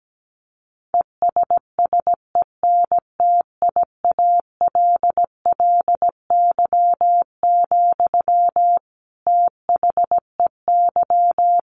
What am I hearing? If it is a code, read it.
Morse code, heard as a keyed tone, about 17 wpm, 700 Hz. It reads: ESSENTIALLY, THEY
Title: Sentences from Top 1300 Words 17wpm